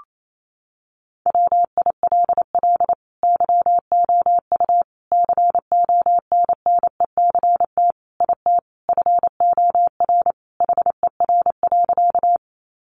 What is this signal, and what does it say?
Morse code, heard as a keyed tone, about 28 wpm, 700 Hz. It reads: WILL YOU CONNECT IT FOR HER.
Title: Sentences from Top 1500 Words 28wpm